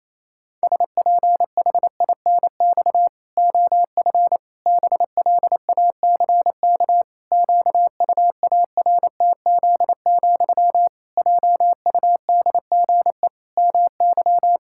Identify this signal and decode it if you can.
Morse code, heard as a keyed tone, about 28 wpm, 700 Hz. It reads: SPHINX OF BLACK QUARTZ, JUDGE MY
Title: Pangram Challenge v2 - 28wpm